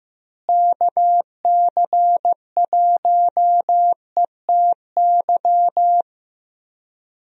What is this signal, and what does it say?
Morse code, heard as a keyed tone, about 15 wpm, 700 Hz. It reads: KC1ETY
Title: USA Call Signs - 6 Characters - Repeated 3x - 15wpm